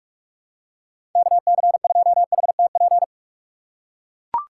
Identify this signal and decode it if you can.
Morse code, heard as a keyed tone, about 43 wpm, 700 Hz. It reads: KC2HTP
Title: USA Call Signs - 6 Characters - 6x Character Spacing - 45wpm